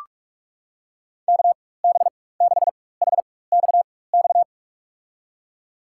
Morse code, heard as a keyed tone, about 44 wpm, spaced as for 21 wpm, 700 Hz. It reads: KB6HXX
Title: USA Call Signs - 6 Characters - 4x Character Spacing - 45wpm